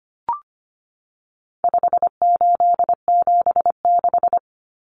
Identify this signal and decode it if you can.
Morse code, heard as a keyed tone, about 25 wpm, 700 Hz. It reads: 5876
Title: Numbers - 4-Digit 25wpm